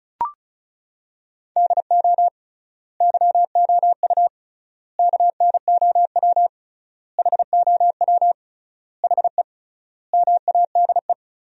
Morse code, heard as a keyed tone, about 34 wpm, 700 Hz. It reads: DO YOU KNOW HOW HE MADE
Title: 3x Word-Spacing - Sentences from Top 300 Words - 35wpm